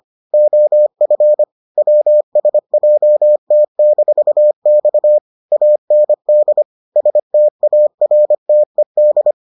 Morse code, heard as a keyed tone, about 25 wpm, 600 Hz. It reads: OF WSJT-X AND STARTED